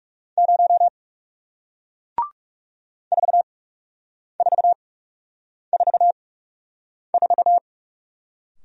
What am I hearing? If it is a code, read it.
Morse code, heard as a keyed tone, about 40 wpm, spaced as for 8 wpm, 700 Hz. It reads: 0 4444